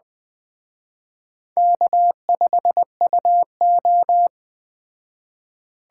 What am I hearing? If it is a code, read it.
Morse code, heard as a keyed tone, about 20 wpm, 700 Hz. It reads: K5UO
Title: Speed Racing - USA 1x2 Call Signs 17wpm